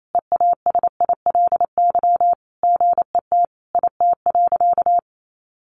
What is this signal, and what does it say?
Morse code, heard as a keyed tone, about 28 wpm, 700 Hz. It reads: EASILY GET IT.
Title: Sentences from Top 1500 Words 28wpm